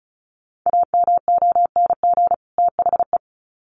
Morse code, heard as a keyed tone, about 34 wpm, 700 Hz. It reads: AMONG THE
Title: Sets of 2 Words 35wpm